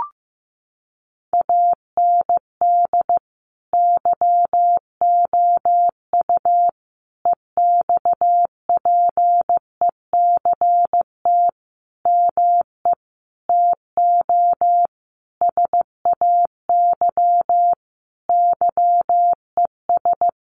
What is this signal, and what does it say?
Morse code, heard as a keyed tone, about 14 wpm, 700 Hz. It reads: AND YOU EXPECT ME TO SAY YES
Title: Sentences from Top 1100 Words 15wpm